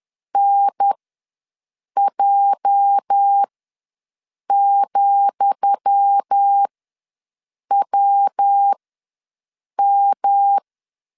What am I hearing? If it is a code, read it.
Morse code, heard as a keyed tone, about 10 wpm, spaced as for 6 wpm, 800 Hz. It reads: NJ,WM
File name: LEZIONI / 10 SIGNS / 10D COMA ws 10wpm 4314